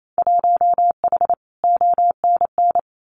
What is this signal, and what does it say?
Morse code, heard as a keyed tone, about 28 wpm, 700 Hz. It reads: 1H ONN
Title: Winter Field Day Exchanges 28wpm